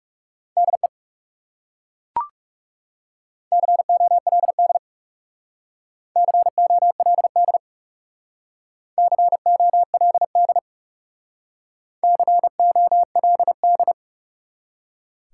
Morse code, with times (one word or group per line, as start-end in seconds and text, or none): 0.6-0.9 DE
3.5-4.8 COLD
6.2-7.6 COLD
9.0-10.6 COLD
12.0-13.9 COLD